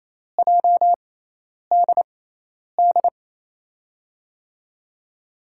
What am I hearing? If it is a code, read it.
Morse code, heard as a keyed tone, about 28 wpm, spaced as for 10 wpm, 700 Hz. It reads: JDD
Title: USA Call Signs - 6 Characters - 6x Character Spacing - 28wpm